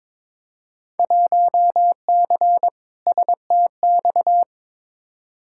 Morse code, heard as a keyed tone, about 22 wpm, 700 Hz. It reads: 1C STX
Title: Field Day Exchanges Encore - 22wpm